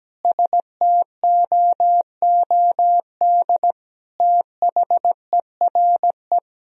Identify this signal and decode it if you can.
Morse code, heard as a keyed tone, about 17 wpm, 700 Hz. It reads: STOOD THERE
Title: Sentences from Top 1000 Words - Review Part 2 - 17wpm